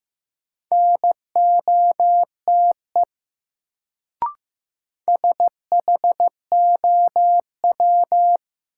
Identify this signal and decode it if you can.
Morse code, heard as a keyed tone, about 15 wpm, 700 Hz. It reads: NOTE SHOW
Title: Sets of 5 Words - New Word Drill - 15wpm